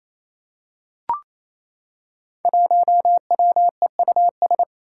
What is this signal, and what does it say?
Morse code, heard as a keyed tone, about 28 wpm, 700 Hz. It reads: 1WEUS
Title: Five Challenging Letter-Number Combinations 28wpm